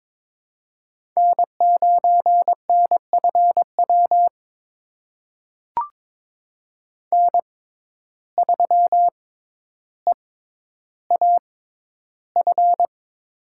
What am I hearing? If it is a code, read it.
Morse code, heard as a keyed tone, about 22 wpm, spaced as for 11 wpm, 700 Hz. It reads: N9NFW N3EAF
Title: USA Call Signs - 5 Characters - 6x Character Spacing - 22wpm